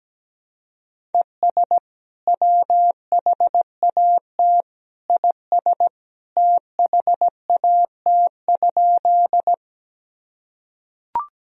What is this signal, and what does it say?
Morse code, heard as a keyed tone, about 17 wpm, 700 Hz. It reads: ES WHAT IS THAT?
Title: Sentences from Top 100 Words and Abbreviations 17wpm